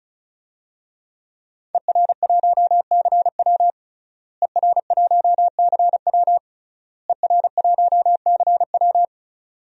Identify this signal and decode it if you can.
Morse code, heard as a keyed tone, about 34 wpm, 700 Hz. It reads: ER1CW ER1CW ER1CW
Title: DXCC-100 Call Signs - 5 Characters - Repeated 3x - 35wpm